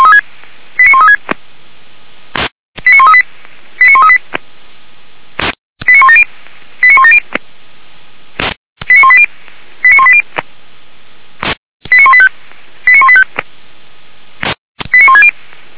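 Five-tone selective call (ZVEI), heard as quick runs of five tones, tones of about 70 ms. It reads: 89136 89138 89138 89170 89170 89180 89180 89185 89185 89160
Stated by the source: ZVEI five-tone selective call received by SDR, address 89135, from Feuerwehr Orlamünde fire brigade server.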